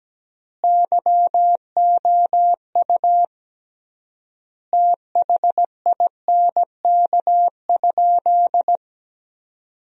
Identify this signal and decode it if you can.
Morse code, heard as a keyed tone, about 17 wpm, 700 Hz. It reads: YOU THINK?
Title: 3x Word-Spacing - Sentences from Top 200 Words - 17wpm